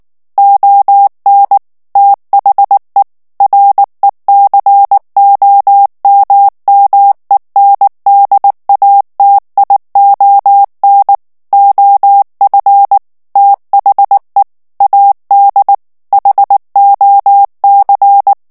Morse code, heard as a keyed tone, about 19 wpm, 800 Hz. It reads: ON THE RECOMMENDATION OF THE AD HOC
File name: ANGLAIS texte clair 3 vitesse 18 mots minute